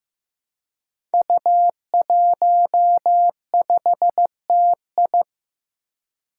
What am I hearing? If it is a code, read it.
Morse code, heard as a keyed tone, about 15 wpm, 700 Hz. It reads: U15TI
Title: Five Challenging Letter-Number Combinations 15wpm